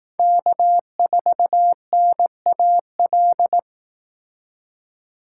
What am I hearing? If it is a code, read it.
Morse code, heard as a keyed tone, about 18 wpm, 700 Hz. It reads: K4NAL
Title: Speed Racing - USA 1x3 Call Signs 15wpm